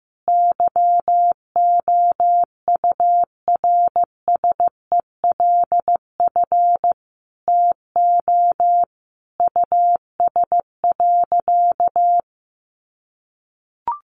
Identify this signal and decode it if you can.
Morse code, heard as a keyed tone, about 15 wpm, 700 Hz. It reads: YOURSELF TO US.
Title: Sentences from Top 600 Words 15wpm